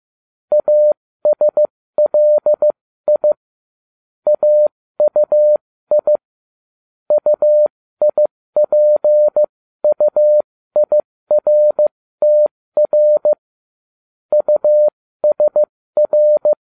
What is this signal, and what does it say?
Morse code, heard as a keyed tone, about 15 wpm, spaced as for 13 wpm, 600 Hz. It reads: ASLI AUI UIPUIRTR USR